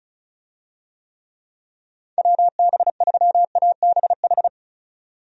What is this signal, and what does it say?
Morse code, heard as a keyed tone, about 36 wpm, 700 Hz. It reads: WB3ABH
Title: USA Call Signs - 6 Characters - 35wpm